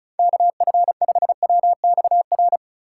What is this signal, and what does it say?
Morse code, heard as a keyed tone, about 36 wpm, 700 Hz. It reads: KF5WXR
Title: USA Call Signs - 3x Character Spacing - 35wpm